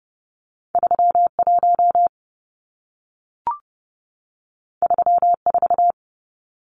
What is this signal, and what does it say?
Morse code, heard as a keyed tone, about 30 wpm, 700 Hz. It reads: 31 34
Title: Numbers - 2-Digit 30wpm